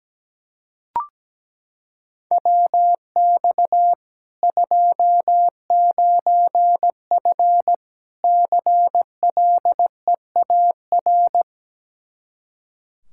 Morse code, heard as a keyed tone, about 17 wpm, 700 Hz. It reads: WX 29F CLEAR
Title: Practice QSO - Exchange 2 - Info - 17wpm